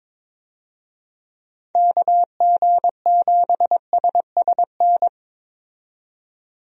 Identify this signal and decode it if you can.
Morse code, heard as a keyed tone, about 22 wpm, 700 Hz. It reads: KG7SSN
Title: USA Call Signs - 6 Characters - 22wpm